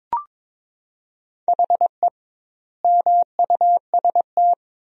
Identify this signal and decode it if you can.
Morse code, heard as a keyed tone, about 22 wpm, 700 Hz. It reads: HE MUST
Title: 2x Word-Spacing - Sentences from Top 500 Words - 22wpm